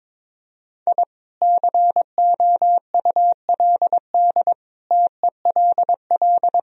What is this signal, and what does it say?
Morse code, heard as a keyed tone, about 22 wpm, 700 Hz. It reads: I COULD TELL
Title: Sentences from Top 1000 Words - Review Part 2 - 22wpm